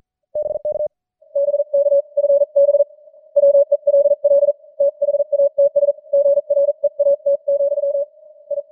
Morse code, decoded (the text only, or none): BK BKFB WELL THATS GREAT, I